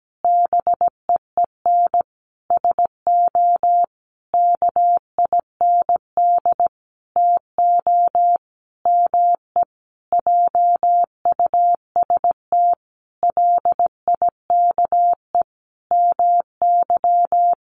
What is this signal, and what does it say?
Morse code, heard as a keyed tone, about 17 wpm, 700 Hz. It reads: BEEN SO KIND TO ME JUST LIKE MY
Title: Sentences from Top 1200 Words 17wpm